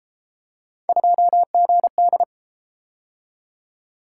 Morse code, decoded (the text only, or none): 2GD